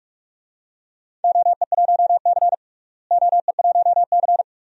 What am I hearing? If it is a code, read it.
Morse code, heard as a keyed tone, about 42 wpm, 700 Hz. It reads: OE1C OE1C
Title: DXCC-100 Call Signs - 4 Characters - Repeated 2x - 45wpm